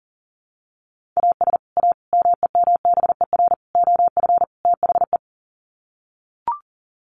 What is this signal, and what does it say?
Morse code, heard as a keyed tone, about 40 wpm, 700 Hz. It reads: AS A MEMBER OF THE
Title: Sets of 5 Words 40wpm